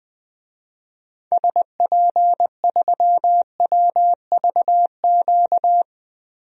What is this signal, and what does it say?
Morse code, heard as a keyed tone, about 20 wpm, 700 Hz. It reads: SP3WVQ
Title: DXCC-100 Call Signs - 6 Characters - 20wpm